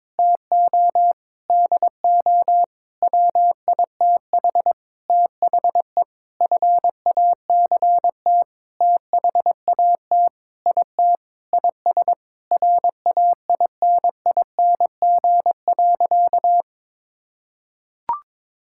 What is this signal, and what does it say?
Morse code, heard as a keyed tone, about 22 wpm, 700 Hz. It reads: TO DO WITH THE FACT THAT IT IS RAINING.
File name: practice / 22wpm / Sentences with 8-Word Phrases - 22wpm